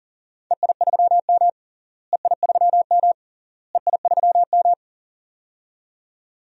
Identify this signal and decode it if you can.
Morse code, heard as a keyed tone, about 37 wpm, 700 Hz. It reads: EI3M EI3M EI3M
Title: DXCC-100 Call Signs - 4 Characters - Repeated 3x - 40wpm